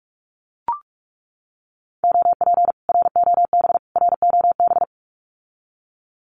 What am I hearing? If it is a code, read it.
Morse code, heard as a keyed tone, about 44 wpm, 700 Hz. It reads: OP ROB ROB
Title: Common QSO Phrases Encore 45wpm